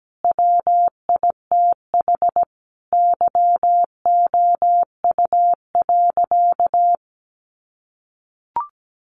Morse code, heard as a keyed tone, about 17 wpm, 700 Hz. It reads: WITH YOU.
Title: Sentences from Top 1500 Words 17wpm